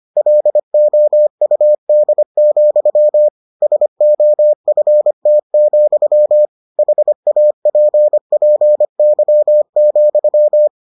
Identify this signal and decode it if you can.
Morse code, heard as a keyed tone, about 25 wpm, 600 Hz. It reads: LOUD, SOFT, HAPPY,